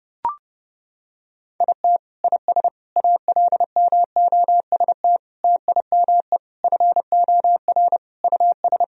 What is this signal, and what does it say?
Morse code, heard as a keyed tone, about 30 wpm, 700 Hz. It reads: IT IS ALMOST TIME FOR US
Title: Sentences from Top 1000 Words - Review Part 4 - 30wpm